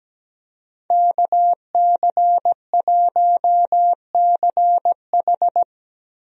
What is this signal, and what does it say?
Morse code, heard as a keyed tone, about 17 wpm, 700 Hz. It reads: KC1CH
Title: USA Call Signs - 5 Characters - Repeated 3x - 17wpm